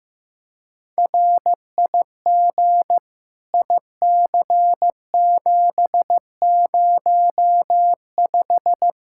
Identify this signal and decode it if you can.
Morse code, heard as a keyed tone, about 15 wpm, 700 Hz. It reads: RIG IC705
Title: Practice QSO - Exchange 2 - Info - 15wpm